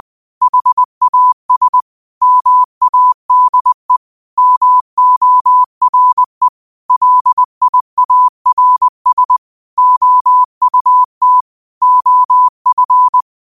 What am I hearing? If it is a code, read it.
Morse code, heard as a keyed tone, about 20 wpm, 1000 Hz. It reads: HAS MADE MORE LIARS OUT OF